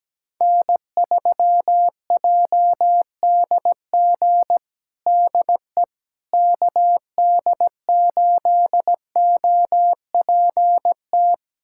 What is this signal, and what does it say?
Morse code, heard as a keyed tone, about 17 wpm, 700 Hz. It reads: N3JDG DE KD8OPT
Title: USA Callsign DE Callsign 17wpm